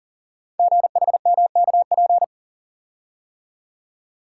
Morse code, heard as a keyed tone, about 44 wpm, 700 Hz. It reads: GHMKP
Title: Code Groups - 5 Letters - 40wpm